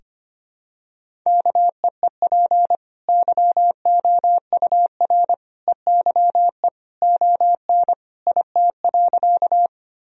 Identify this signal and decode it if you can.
Morse code, heard as a keyed tone, about 25 wpm, 700 Hz. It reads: KEEP YOUR EYE ON IT.
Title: Sentences from Top 1000 Words - Review Part 1 - 25wpm